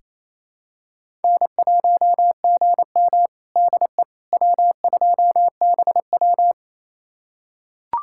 Morse code, heard as a keyed tone, about 28 wpm, 700 Hz. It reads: N1GM DE W2BW
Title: USA Callsign DE Callsign - 4 Characters - 28wpm